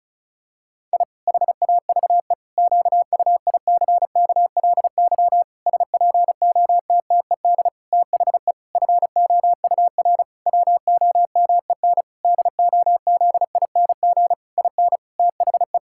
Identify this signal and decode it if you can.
Morse code, heard as a keyed tone, about 34 wpm, 700 Hz. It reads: I HAVE QUICKLY SPOTTED THE FOUR WOMEN DOZING IN THE
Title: Pangram Challenge v2 - 35wpm